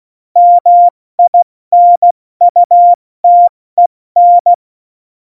Morse code, heard as a keyed tone, about 16 wpm, 700 Hz. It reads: MINUTEN
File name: Morsekurs / Witze / Witz 02. Oktober 2018, 70 F80